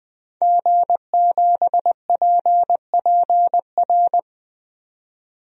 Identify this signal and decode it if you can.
Morse code, heard as a keyed tone, about 20 wpm, 700 Hz. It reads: G7PPR
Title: DXCC-100 Call Signs - 5 Characters - 20wpm